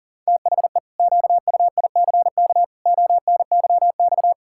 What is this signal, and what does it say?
Morse code, heard as a keyed tone, about 40 wpm, 700 Hz. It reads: THE QUICK ONYX
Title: Pangram Challenge v2 - 40wpm